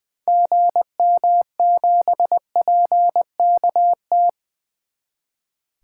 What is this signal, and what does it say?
Morse code, heard as a keyed tone, about 20 wpm, 700 Hz. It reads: GM7PKT
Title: 2023 SOTA Activator Call Signs - 20wpm